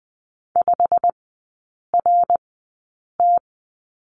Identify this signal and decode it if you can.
Morse code, heard as a keyed tone, about 20 wpm, spaced as for 8 wpm, 690 Hz. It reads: 5RT
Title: 087 Lesson 15e - Number 5 - Three Characters 20wpm